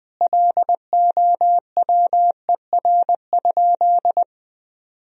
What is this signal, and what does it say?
Morse code, heard as a keyed tone, about 20 wpm, 700 Hz. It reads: LOWER?
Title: Sentences from Top 1000 Words - Review Part 4 - 20wpm